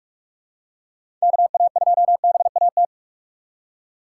Morse code, heard as a keyed tone, about 46 wpm, 700 Hz. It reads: KA2BAT
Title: USA Call Signs - 6 Characters - 6x Character Spacing - 45wpm